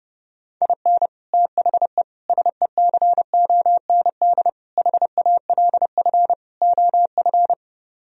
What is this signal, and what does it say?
Morse code, heard as a keyed tone, about 32 wpm, 700 Hz. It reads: IN THE SECOND HALF OF